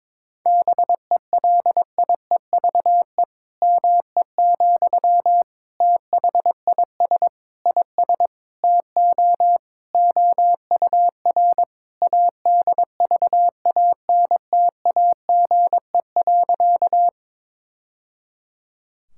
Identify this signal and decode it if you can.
Morse code, heard as a keyed tone, about 22 wpm, 700 Hz. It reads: BELIEVE ME, THIS IS TO OUR ADVANTAGE.
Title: Sentences from Top 1000 Words - Review Part 6 - 22wpm